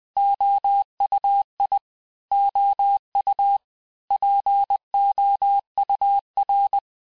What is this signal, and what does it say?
Morse code, heard as a keyed tone, about 20 wpm, 800 Hz. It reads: OUI OU POUR
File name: cours-du-mercredi / audios / cours-858-texte-clais-vitesse-19mpm